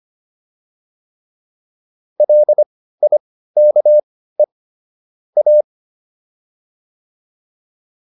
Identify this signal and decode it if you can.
Morse code, heard as a keyed tone, about 25 wpm, spaced as for 15 wpm, 600 Hz. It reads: LIKE A